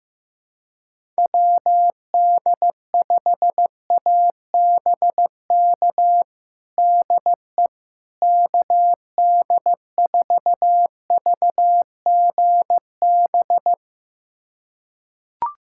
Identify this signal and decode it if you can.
Morse code, heard as a keyed tone, about 15 wpm, 700 Hz. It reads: WD5ABK DE KD4VGB
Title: USA Callsign DE Callsign 15wpm